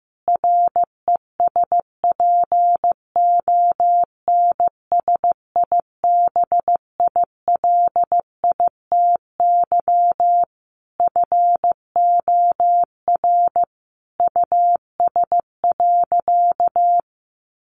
Morse code, heard as a keyed tone, about 15 wpm, 700 Hz. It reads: RESPONSIBILITY FOR US.
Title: Sentences from Top 1500 Words 15wpm